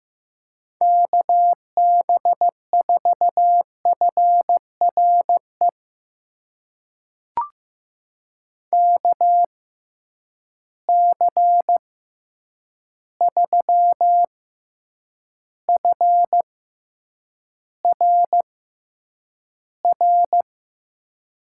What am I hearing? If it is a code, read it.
Morse code, heard as a keyed tone, about 15 wpm, spaced as for 8 wpm, 700 Hz. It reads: KB4FRE KC3FRR
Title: USA Call Signs - 6 Characters - 6x Character Spacing - 15wpm